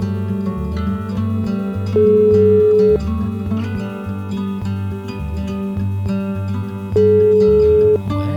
Ringback tone (Europe, 425 Hz), with noise.